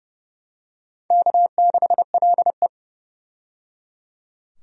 Morse code, heard as a keyed tone, about 29 wpm, 700 Hz. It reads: K6LE
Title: USA Call Signs - 4 Characters - 30wpm